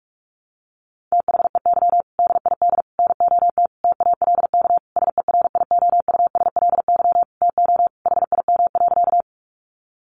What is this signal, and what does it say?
Morse code, heard as a keyed tone, about 44 wpm, 700 Hz. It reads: THEY DID NOT TALK SERIOUSLY TO HIM.
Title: Sentences from Top 1200 Words 45wpm